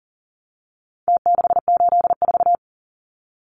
Morse code, heard as a keyed tone, about 40 wpm, 700 Hz. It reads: T684
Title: Numbers - Contest Style - 4-Digit 40wpm